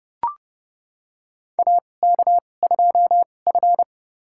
Morse code, heard as a keyed tone, about 30 wpm, spaced as for 22 wpm, 700 Hz. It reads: AK2F
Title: USA Call Signs - 2x Character Spacing - 30wpm